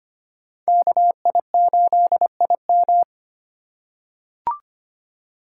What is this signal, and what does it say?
Morse code, heard as a keyed tone, about 25 wpm, 700 Hz. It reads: KI8IM
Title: USA Call Signs - 5 Characters - 2x Character Spacing - 25wpm